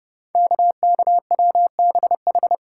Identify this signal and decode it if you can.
Morse code, heard as a keyed tone, about 30 wpm, 700 Hz. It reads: KKWBH
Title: Code Groups - 5 Letters - 30wpm